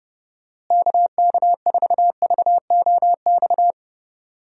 Morse code, heard as a keyed tone, about 30 wpm, 700 Hz. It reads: KK4VOX
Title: USA Call Signs - 6 Characters - Repeated 2x - 30wpm